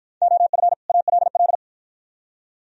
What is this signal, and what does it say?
Morse code, heard as a keyed tone, about 52 wpm, 700 Hz. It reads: OF ALL